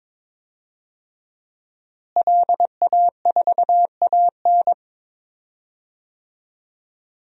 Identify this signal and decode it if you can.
Morse code, heard as a keyed tone, about 22 wpm, 700 Hz. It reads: LA4AN
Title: DXCC-100 Call Signs - 5 Characters - Repeated 2x - 22wpm